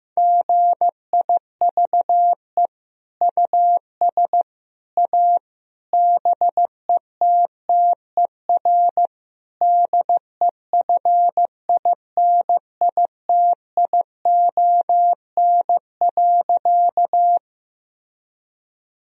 Morse code, heard as a keyed tone, about 15 wpm, 700 Hz. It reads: GIVE US A BETTER DEFINITION.
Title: Sentences from Top 1000 Words - Review Part 4 - 15wpm